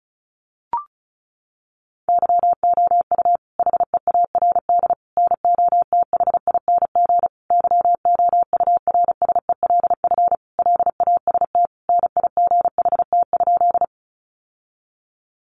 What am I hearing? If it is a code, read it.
Morse code, heard as a keyed tone, about 34 wpm, 700 Hz. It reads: YOU HEARD NOTHING YOURSELF LAST NIGHT?
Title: Sentences from Top 600 Words 35wpm